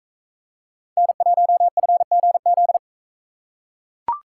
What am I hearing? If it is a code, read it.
Morse code, heard as a keyed tone, about 44 wpm, 700 Hz. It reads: N1FGZ